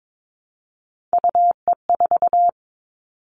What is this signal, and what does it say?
Morse code, heard as a keyed tone, about 22 wpm, 700 Hz. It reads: UE4
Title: Three Challenging Letter-Number Combinations 22wpm